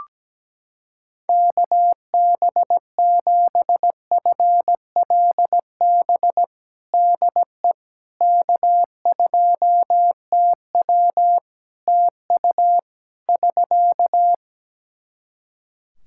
Morse code, heard as a keyed tone, about 17 wpm, 700 Hz. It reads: KB7FLB DE K2TW TU <SK>
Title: Practice QSO - Exchange 4 - Ending - 17wpm